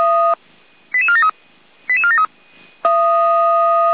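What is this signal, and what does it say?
Five-tone selective call (ZVEI), heard as quick runs of five tones, tones of about 70 ms. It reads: 88473 88473
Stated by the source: ZVEI five-tone selective call received on an SDR.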